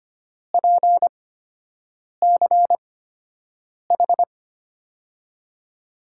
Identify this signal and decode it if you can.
Morse code, heard as a keyed tone, about 25 wpm, spaced as for 7 wpm, 700 Hz. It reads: PCH
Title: USA Call Signs - 6 Characters - 8x Character Spacing - 25wpm